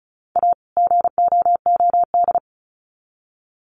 Morse code, heard as a keyed tone, about 34 wpm, 700 Hz. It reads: A GOOD
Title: Sets of 2 Words 35wpm